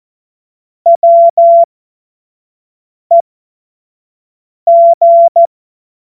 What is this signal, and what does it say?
Morse code, heard as a keyed tone, about 14 wpm, spaced as for 5 wpm, 680 Hz. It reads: WEG
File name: Witz-09-September-2025-25-F70